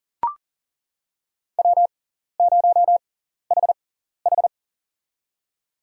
Morse code, heard as a keyed tone, about 40 wpm, spaced as for 14 wpm, 690 Hz. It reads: W0HH
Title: USA Call Signs - 6x Character Spacing - 40wpm